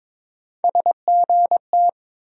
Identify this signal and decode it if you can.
Morse code, heard as a keyed tone, about 22 wpm, 700 Hz. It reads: SGT
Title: USA Call Sign Postfixes - 3 Letters 22wpm